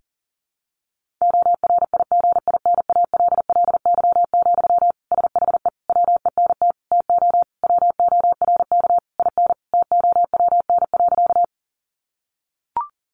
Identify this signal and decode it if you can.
Morse code, heard as a keyed tone, about 40 wpm, 700 Hz. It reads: ORIGINALLY, SHE WENT TO WORK IN TOWN.
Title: Sentences from Top 1500 Words 40wpm